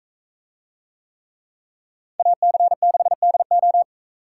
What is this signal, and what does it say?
Morse code, heard as a keyed tone, about 40 wpm, 700 Hz. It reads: AC6DO